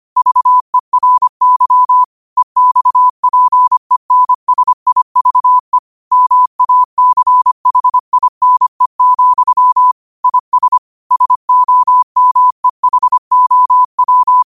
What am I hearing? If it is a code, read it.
Morse code, heard as a keyed tone, about 25 wpm, 1000 Hz. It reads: VERY EXPENSIVE MACHINE, IS SOMEHOW